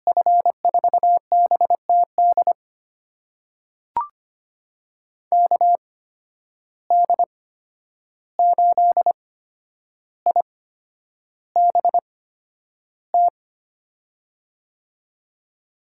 Morse code, heard as a keyed tone, about 25 wpm, spaced as for 10 wpm, 700 Hz. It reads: F4BTD KD8IBT
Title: USA Call Signs - 6 Characters - 8x Character Spacing - 25wpm